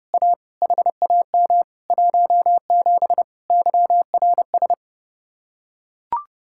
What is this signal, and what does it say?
Morse code, heard as a keyed tone, about 30 wpm, 700 Hz. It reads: A HAM 17 YRS